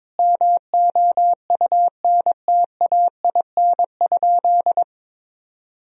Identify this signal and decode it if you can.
Morse code, heard as a keyed tone, about 22 wpm, 700 Hz. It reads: MOUNTAIN?
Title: Sentences from Top 1000 Words - Review Part 1 - 22wpm